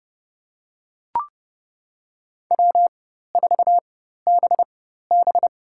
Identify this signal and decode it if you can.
Morse code, heard as a keyed tone, about 32 wpm, spaced as for 14 wpm, 700 Hz. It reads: W4BB
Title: USA Call Signs - 4 Characters - 4x Character Spacing - 30wpm